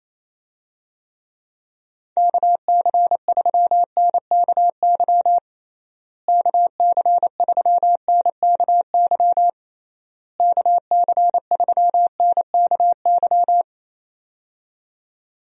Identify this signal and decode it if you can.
Morse code, heard as a keyed tone, about 28 wpm, 700 Hz. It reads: KC3NKY KC3NKY KC3NKY
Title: USA Call Signs - 6 Characters - Repeated 3x - 28wpm